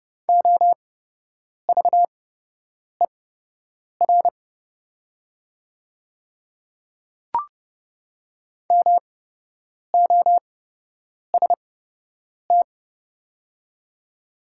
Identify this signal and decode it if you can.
Morse code, heard as a keyed tone, about 29 wpm, spaced as for 8 wpm, 700 Hz. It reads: OVER MOST
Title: 8x Character Spacing - Top 100 Words - 30wpm